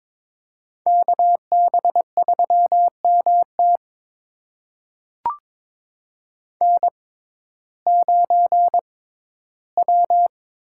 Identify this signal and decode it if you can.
Morse code, heard as a keyed tone, about 23 wpm, spaced as for 14 wpm, 700 Hz. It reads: KB3MT N9W
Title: USA Call Signs - 5 Characters - 6x Character Spacing - 22wpm